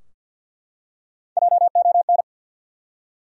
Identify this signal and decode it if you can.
Morse code, heard as a keyed tone, about 53 wpm, 700 Hz. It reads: JON
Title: USA Common Names - Max 3 Letters - 50wpm